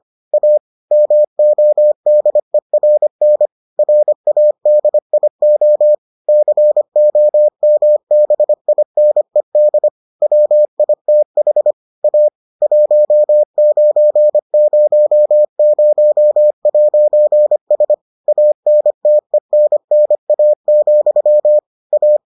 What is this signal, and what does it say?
Morse code, heard as a keyed tone, about 25 wpm, 600 Hz. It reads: A MODERN RADIO COMBINED WITH A 1900'S ANTENNA, A